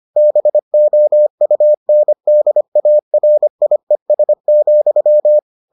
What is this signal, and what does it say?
Morse code, heard as a keyed tone, about 25 wpm, 600 Hz. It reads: BOUNDARIES,